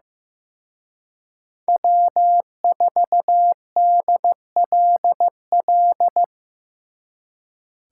Morse code, heard as a keyed tone, about 18 wpm, 700 Hz. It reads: W4DLL
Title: Speed Racing - USA Call Signs 15wpm